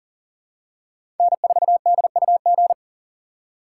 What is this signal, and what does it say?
Morse code, heard as a keyed tone, about 40 wpm, 700 Hz. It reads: N4DUG
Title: USA Call Signs - 5 Characters - 2x Character Spacing - 40wpm